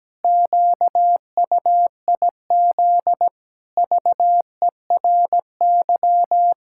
Morse code, heard as a keyed tone, about 17 wpm, 700 Hz. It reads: QUIZ VERY
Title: Pangram Challenge v2 - 17wpm